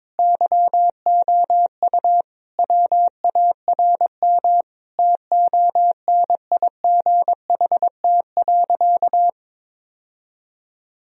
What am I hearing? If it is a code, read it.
Morse code, heard as a keyed tone, about 22 wpm, 700 Hz. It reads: YOU WARM TONIGHT.
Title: Sentences from Top 1000 Words - Review Part 3 - 22wpm